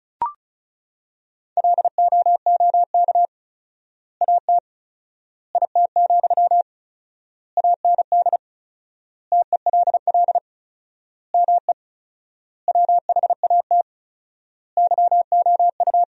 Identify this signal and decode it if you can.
Morse code, heard as a keyed tone, about 36 wpm, 700 Hz. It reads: LOOK AT IT, AND TELL ME WHAT YOU
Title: 4x Word-Spacing - Sentences from Top 200 Words - 35wpm